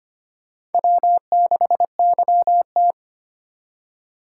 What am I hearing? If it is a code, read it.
Morse code, heard as a keyed tone, about 33 wpm, 700 Hz. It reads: W6YT